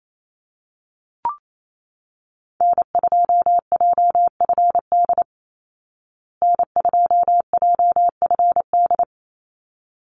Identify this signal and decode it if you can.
Morse code, heard as a keyed tone, about 28 wpm, 690 Hz. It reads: N2JFD N2JFD
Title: Call Signs Repeated 3x - 28wpm